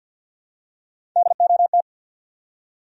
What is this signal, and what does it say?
Morse code, heard as a keyed tone, about 53 wpm, 700 Hz. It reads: DOT
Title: USA Common Names - Max 5 Letters - 50wpm